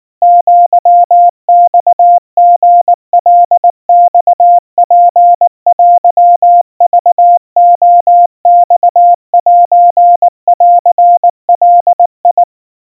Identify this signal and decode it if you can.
Morse code (Japanese Wabun code), heard as a keyed tone, about 19 wpm, 700 Hz. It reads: アマリカマツテクレマセンガ
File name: kawa / Morse sample / kamyu 70 cpm 700 Hz Crispy